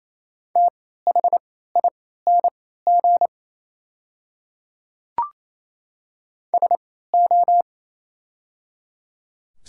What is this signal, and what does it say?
Morse code, heard as a keyed tone, about 27 wpm, spaced as for 16 wpm, 700 Hz. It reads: THING SO
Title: 3x Character Spacing - Top 100 Words - 28wpm